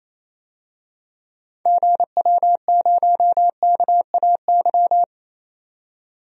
Morse code, heard as a keyed tone, about 28 wpm, 700 Hz. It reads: GW0KAY
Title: DXCC-100 Call Signs - 6 Characters - 28wpm